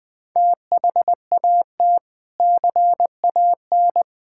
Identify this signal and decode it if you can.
Morse code, heard as a keyed tone, about 21 wpm, 700 Hz. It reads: THAT CAN
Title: Build A Phrase - 3-Word Phrases - 20wpm